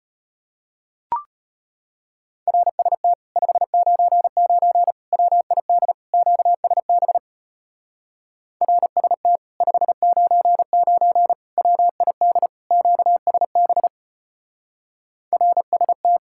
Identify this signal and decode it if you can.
Morse code, heard as a keyed tone, about 33 wpm, 700 Hz. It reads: RST 599 WID QSB RST 599 WID QSB RST